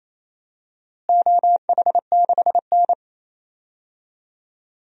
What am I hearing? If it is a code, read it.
Morse code, heard as a keyed tone, about 28 wpm, 700 Hz. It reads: OH6N
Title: DXCC-100 Call Signs - 4 Characters - Repeated 3x - 28wpm